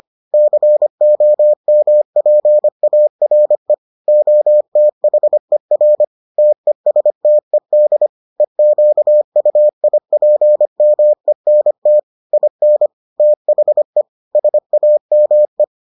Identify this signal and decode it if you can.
Morse code, heard as a keyed tone, about 25 wpm, 600 Hz. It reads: COMPARE OTHER TESTED EQUIPMENT IN THE SAME